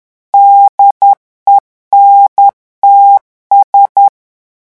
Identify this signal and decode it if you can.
Morse code, heard as a keyed tone, about 10 wpm, 800 Hz. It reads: DENTS
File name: revision9 citations humoristiques vitesse 10 mots minute